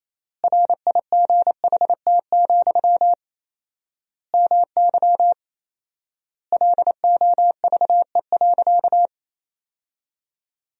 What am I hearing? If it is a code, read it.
Morse code, heard as a keyed tone, about 28 wpm, 700 Hz. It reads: RIGHT, MY LOVE.
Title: 4x Word-Spacing - Sentences from Top 400 Words - 28wpm